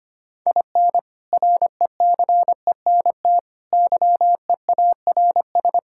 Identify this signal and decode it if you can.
Morse code, heard as a keyed tone, about 25 wpm, 700 Hz. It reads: IN RECENT YEARS
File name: practice / 25wpm / Sentences with 3-Word Phrases - 25wpm